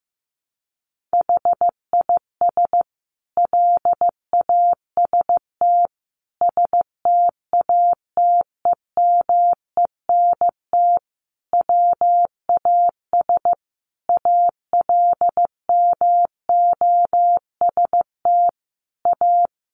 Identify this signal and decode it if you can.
Morse code, heard as a keyed tone, about 15 wpm, 700 Hz. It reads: HIS LAST STATEMENT WAS ALMOST A
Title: Sentences from Top 1200 Words 15wpm